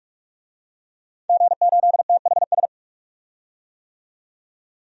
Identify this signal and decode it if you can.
Morse code, heard as a keyed tone, about 44 wpm, 690 Hz. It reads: G8THS
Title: DXCC-100 Call Signs - 5 Characters - 45wpm